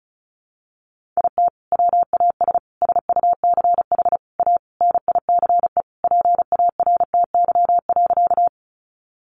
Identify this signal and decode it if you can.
Morse code, heard as a keyed tone, about 34 wpm, 700 Hz. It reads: IT WAS SUCH A NICE PARTY.
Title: Sentences from Top 1400 Words 35wpm